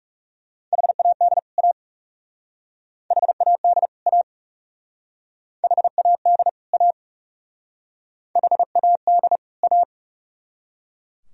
Morse code, heard as a keyed tone, about 39 wpm, 700 Hz. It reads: HAD A HAD A HAD A HAD A